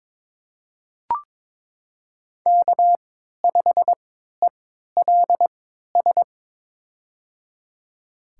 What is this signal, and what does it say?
Morse code, heard as a keyed tone, about 22 wpm, spaced as for 13 wpm, 700 Hz. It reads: K5ELS
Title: USA Call Signs - 5 Characters - 3x Character Spacing - 22wpm